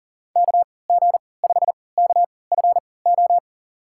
Morse code, heard as a keyed tone, about 37 wpm, spaced as for 22 wpm, 700 Hz. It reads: KG5KFO
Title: USA Call Signs - 6 Characters - 3x Character Spacing - 40wpm